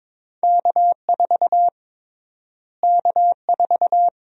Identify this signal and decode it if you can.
Morse code, heard as a keyed tone, about 22 wpm, 700 Hz. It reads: K4 K4
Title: USA Call Sign Prefixes - Repeated 2x - 22wpm